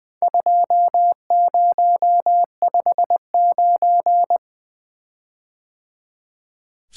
Morse code, heard as a keyed tone, about 20 wpm, 700 Hz. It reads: 2059
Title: Four Letter-Number Combinations - 20wpm